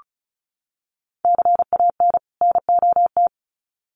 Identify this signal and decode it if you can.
Morse code, heard as a keyed tone, about 34 wpm, 700 Hz. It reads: CAN NOT
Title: Sets of 2 Words 35wpm